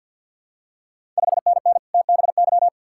Morse code, heard as a keyed tone, about 51 wpm, 700 Hz. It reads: 5NN TBY
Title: Ontario QSO Party Exchange 50wpm